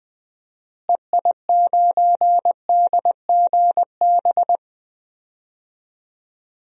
Morse code, heard as a keyed tone, about 20 wpm, 700 Hz. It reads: EI9DGB
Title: DXCC-100 Call Signs - 6 Characters - Repeated 2x - 20wpm